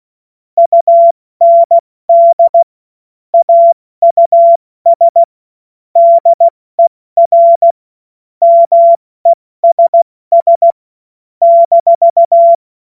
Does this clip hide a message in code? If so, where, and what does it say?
Yes: Morse code from 0.6 to 12.5 s, heard as a keyed tone, about 16 wpm, spaced as for 14 wpm, 680 Hz. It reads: UND AUS DER MESS -